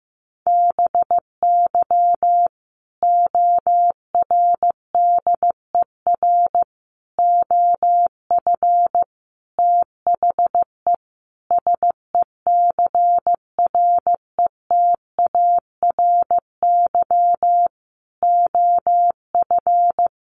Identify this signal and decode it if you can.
Morse code, heard as a keyed tone, about 15 wpm, 700 Hz. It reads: BY ORDER OF THE SECRETARY OF
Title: Sentences from Top 1300 Words 15wpm